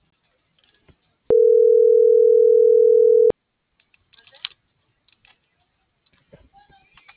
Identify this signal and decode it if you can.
Ringback tone (North America), 440 + 480 Hz.